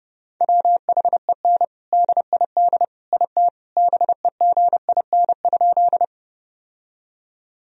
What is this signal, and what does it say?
Morse code, heard as a keyed tone, about 30 wpm, 700 Hz. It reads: WHEN DID IT BEGIN?
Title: Sentences from Top 1000 Words - Review Part 1 - 30wpm